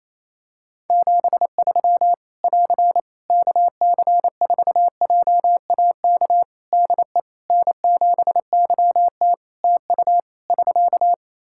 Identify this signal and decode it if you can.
Morse code, heard as a keyed tone, about 28 wpm, 700 Hz. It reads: 73 <AR> KC4JAK DE N7YT TU <SK>
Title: Practice QSO - Exchange 4 - Ending - 28wpm